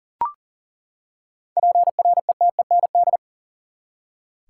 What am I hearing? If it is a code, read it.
Morse code, heard as a keyed tone, about 40 wpm, 700 Hz. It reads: PRETEND
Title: Seven Letter Words - 40wpm